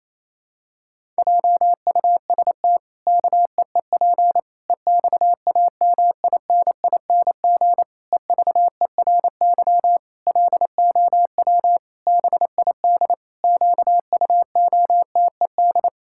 Morse code, heard as a keyed tone, about 28 wpm, 700 Hz. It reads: JUST KEEP EXAMINING EVERY LOW BID QUOTED
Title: Pangram Challenge v2 - 28wpm